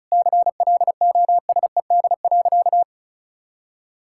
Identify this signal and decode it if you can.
Morse code, heard as a keyed tone, about 36 wpm, 700 Hz. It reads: CLOSED.
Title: Sentences from Top 1600 Words 35wpm